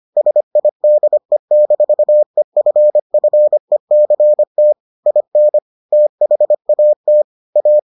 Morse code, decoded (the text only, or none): SIDE-EFFECT IN THAT A